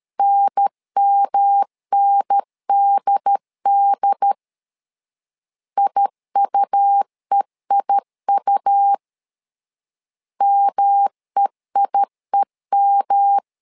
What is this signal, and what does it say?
Morse code, heard as a keyed tone, about 13 wpm, 800 Hz. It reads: NMNDD IUEIU MEIEM
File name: LEZIONI / 3 MWRO / 3 M 12wpm 1541